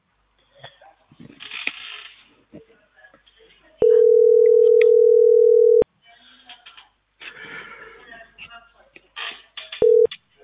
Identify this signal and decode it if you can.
Ringback tone (North America), 440 + 480 Hz.